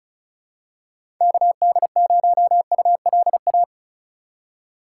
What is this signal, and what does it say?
Morse code, heard as a keyed tone, about 34 wpm, 700 Hz. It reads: KD0ULA
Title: USA Call Signs - 6 Characters - 3x Character Spacing - 35wpm